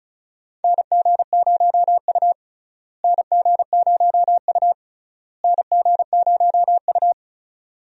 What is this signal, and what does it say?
Morse code, heard as a keyed tone, about 34 wpm, 700 Hz. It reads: NG0U NG0U NG0U
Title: USA Call Signs - 4 Characters - Repeated 3x - 35wpm